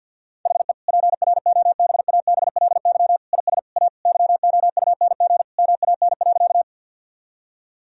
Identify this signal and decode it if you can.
Morse code, heard as a keyed tone, about 50 wpm, 700 Hz. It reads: HE PROBABLY IS A YOUNG MAN.
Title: Sentences from Top 1000 Words - Review Part 4 - 50wpm